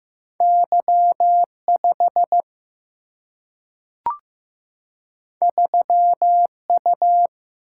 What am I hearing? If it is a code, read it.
Morse code, heard as a keyed tone, about 15 wpm, 700 Hz. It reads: Y5 3U
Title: Single-Letter Single-Number 15wpm